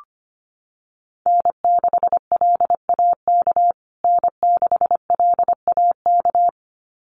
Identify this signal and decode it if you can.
Morse code, heard as a keyed tone, about 25 wpm, 700 Hz. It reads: N6LAK N6LAK
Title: USA Call Signs Repeated 2x 25wpm